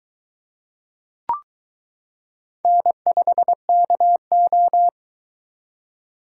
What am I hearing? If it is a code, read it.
Morse code, heard as a keyed tone, about 22 wpm, 700 Hz. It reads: N5KO